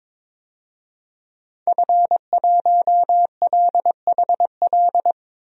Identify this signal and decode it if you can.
Morse code, heard as a keyed tone, about 22 wpm, 700 Hz. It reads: F1LHL
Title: DXCC-100 Call Signs - 5 Characters - 22wpm